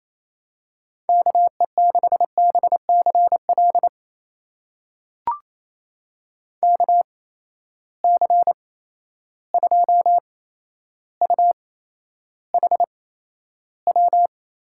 Morse code, heard as a keyed tone, about 28 wpm, spaced as for 12 wpm, 700 Hz. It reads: KE6BCL KC2UHW
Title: USA Call Signs - 6 Characters - 8x Character Spacing - 28wpm